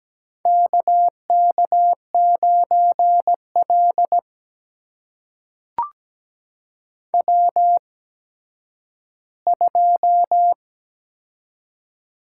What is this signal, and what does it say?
Morse code, heard as a keyed tone, about 17 wpm, spaced as for 10 wpm, 700 Hz. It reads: KK9L W2
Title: USA Call Signs - 8x Character Spacing - 17wpm